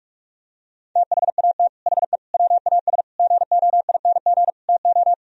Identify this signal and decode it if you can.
Morse code, heard as a keyed tone, about 49 wpm, 700 Hz. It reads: THAT HE WAS GOING TO